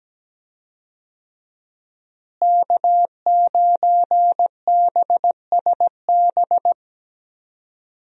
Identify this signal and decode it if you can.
Morse code, heard as a keyed tone, about 17 wpm, 700 Hz. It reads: K9BSB
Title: USA Call Signs - 5 Characters - Repeated 3x - 17wpm